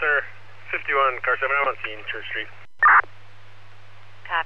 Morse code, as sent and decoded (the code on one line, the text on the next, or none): none